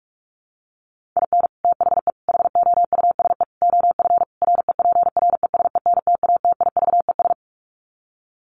Morse code, heard as a keyed tone, about 44 wpm, 700 Hz. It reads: IN THE HOUSE OF REPRESENTATIVES
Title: Sets of 5 Words Encore 45wpm